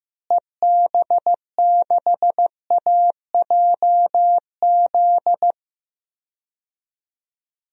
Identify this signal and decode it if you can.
Morse code, heard as a keyed tone, about 15 wpm, 700 Hz. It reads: EB6AJZ
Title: DXCC-100 Call Signs - 6 Characters - Repeated 3x - 15wpm